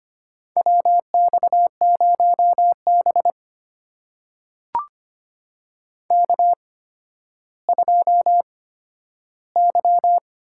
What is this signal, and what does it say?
Morse code, heard as a keyed tone, about 25 wpm, spaced as for 12 wpm, 700 Hz. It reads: WX0B K2Y
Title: USA Call Signs - 8x Character Spacing - 25wpm